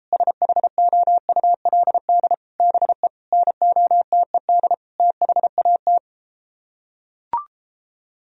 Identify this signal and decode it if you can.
Morse code, heard as a keyed tone, about 34 wpm, 700 Hz. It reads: SHOULD BE NOTED THAT